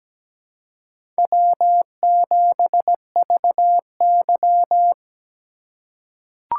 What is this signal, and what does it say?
Morse code, heard as a keyed tone, about 17 wpm, 700 Hz. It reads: W7VY
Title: USA Call Signs - 4 Characters - 3x Character Spacing - 17wpm